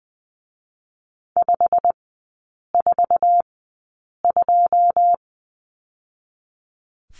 Morse code, heard as a keyed tone, about 20 wpm, spaced as for 8 wpm, 690 Hz. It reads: 542
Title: 087 Lesson 15e - Number 5 - Three Characters 20wpm